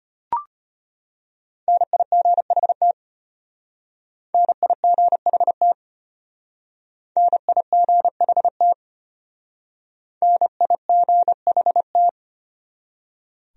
Morse code, heard as a keyed tone, about 33 wpm, 700 Hz. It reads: NIGHT NIGHT NIGHT NIGHT